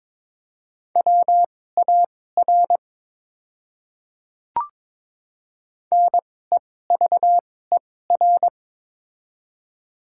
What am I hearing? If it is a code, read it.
Morse code, heard as a keyed tone, about 22 wpm, spaced as for 16 wpm, 700 Hz. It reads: WAR NEVER
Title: 2x Character Spacing - Top 500 Words - 22wpm